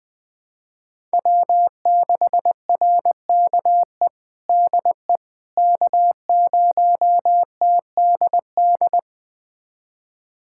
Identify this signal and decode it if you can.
Morse code, heard as a keyed tone, about 20 wpm, 700 Hz. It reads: W6RKE DE K0TDD
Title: USA Callsign DE Callsign - 5 Characters - 20wpm